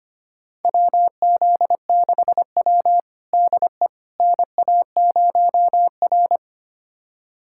Morse code, heard as a keyed tone, about 25 wpm, 700 Hz. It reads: WZ6W DE NA0R
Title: USA Callsign DE Callsign - 4 Characters - 25wpm